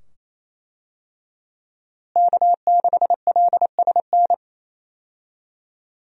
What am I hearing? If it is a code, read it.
Morse code, heard as a keyed tone, about 28 wpm, 700 Hz. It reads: K6LSN
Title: USA Call Signs - 5 Characters - Repeated 2x - 28wpm